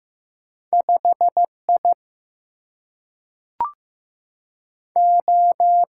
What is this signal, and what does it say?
Morse code, heard as a keyed tone, about 15 wpm, 700 Hz. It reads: 5I O
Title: Single-Letter Single-Number 15wpm